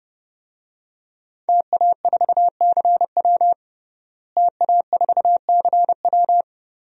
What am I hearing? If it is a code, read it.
Morse code, heard as a keyed tone, about 32 wpm, 700 Hz. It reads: TA4CW TA4CW
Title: DXCC-100 Call Signs - 5 Characters - Repeated 2x - 30wpm